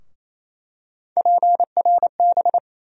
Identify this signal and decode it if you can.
Morse code, heard as a keyed tone, about 28 wpm, 700 Hz. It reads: PRB
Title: USA Call Sign Postfixes - 3 Letters 28wpm